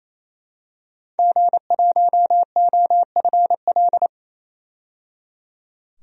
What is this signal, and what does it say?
Morse code, heard as a keyed tone, about 28 wpm, 700 Hz. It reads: G1OFL
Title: DXCC-100 Call Signs - 5 Characters - 28wpm